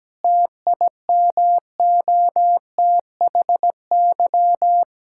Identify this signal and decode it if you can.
Morse code, heard as a keyed tone, about 17 wpm, 700 Hz. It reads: TIMOTHY
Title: Practice QSO - Exchange 2 - Info - 17wpm